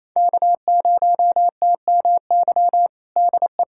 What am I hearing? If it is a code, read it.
Morse code, heard as a keyed tone, about 28 wpm, 700 Hz. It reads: K0TMY DE
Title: USA Callsign DE Callsign 28wpm